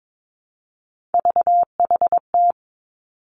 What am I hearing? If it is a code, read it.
Morse code, heard as a keyed tone, about 22 wpm, 700 Hz. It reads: VHT
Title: Three Challenging Letter-Number Combinations 22wpm